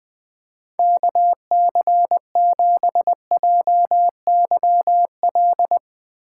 Morse code, heard as a keyed tone, about 20 wpm, 700 Hz. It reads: KC7JYL
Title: Speed Racing - USA Call Signs 20wpm